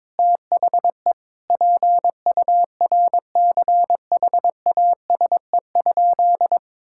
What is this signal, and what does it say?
Morse code, heard as a keyed tone, about 22 wpm, 700 Hz. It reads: THE PURCHASE?
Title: Sentences from Top 1000 Words - Review Part 5 - 22wpm